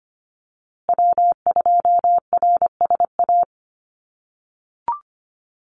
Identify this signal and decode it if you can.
Morse code, heard as a keyed tone, about 25 wpm, 700 Hz. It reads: W2RSA
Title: USA Call Signs Repeated 3x 25wpm